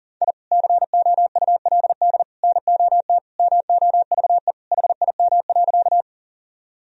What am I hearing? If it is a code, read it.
Morse code, heard as a keyed tone, about 40 wpm, 700 Hz. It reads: I COULD NOT MOVE HIM.
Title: Sentences from Top 1000 Words - Review Part 1 - 40wpm